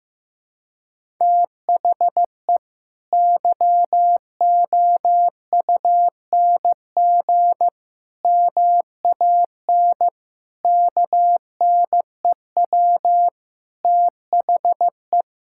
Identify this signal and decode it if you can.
Morse code, heard as a keyed tone, about 14 wpm, 700 Hz. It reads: THE YOUNG MAN KNEW THE
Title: Sentences from Top 1000 Words - Review Part 6 - 15wpm